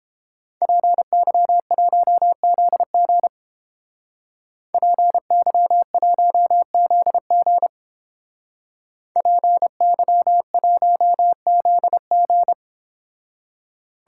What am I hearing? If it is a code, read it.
Morse code, heard as a keyed tone, about 29 wpm, 700 Hz. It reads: PY1ZG PY1ZG PY1ZG